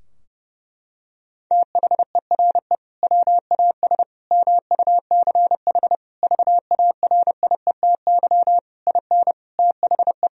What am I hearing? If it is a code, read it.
Morse code, heard as a keyed tone, about 32 wpm, 700 Hz. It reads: THERE WAS MUCH VARIETY IN THE
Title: Sentences from Top 1000 Words - Review Part 4 - 30wpm